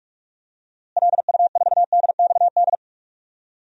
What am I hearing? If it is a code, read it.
Morse code, heard as a keyed tone, about 46 wpm, 700 Hz. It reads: LU4DXD
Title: DXCC-100 Call Signs - 6 Characters - Repeated 3x - 45wpm